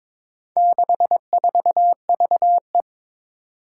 Morse code, heard as a keyed tone, about 22 wpm, 700 Hz. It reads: 64VE
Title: Four Challenging Letter-Number Combinations 22wpm